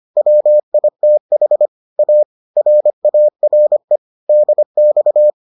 Morse code, heard as a keyed tone, about 25 wpm, 600 Hz. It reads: WITH A RARE DX